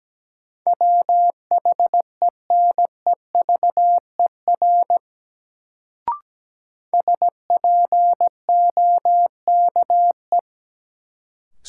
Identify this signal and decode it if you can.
Morse code, heard as a keyed tone, about 17 wpm, 700 Hz. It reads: WHENEVER SPOKE
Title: Sentences from Top 1200 Words - New Word Drill Alt - 17wpm